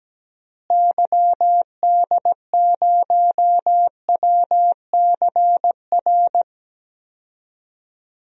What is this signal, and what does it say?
Morse code, heard as a keyed tone, about 17 wpm, 700 Hz. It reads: YD0WCR
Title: DXCC-100 Call Signs - 6 Characters - Repeated 3x - 17wpm